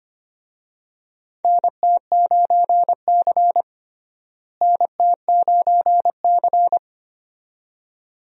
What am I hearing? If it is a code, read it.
Morse code, heard as a keyed tone, about 25 wpm, 700 Hz. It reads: NT9C NT9C
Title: USA Call Signs - 4 Characters - Repeated 2x - 25wpm